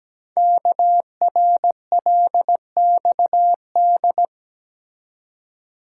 Morse code, heard as a keyed tone, about 17 wpm, 700 Hz. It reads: KRLXD
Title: Code Groups - 5 Letters - 17wpm